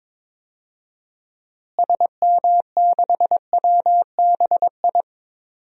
Morse code, heard as a keyed tone, about 22 wpm, 700 Hz. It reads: SM6WBI
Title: DXCC-100 Call Signs - 6 Characters - Repeated 2x - 22wpm